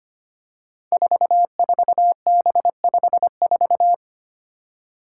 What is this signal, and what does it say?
Morse code, heard as a keyed tone, about 25 wpm, 700 Hz. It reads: 44B54
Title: Five Challenging Letter-Number Combinations 25wpm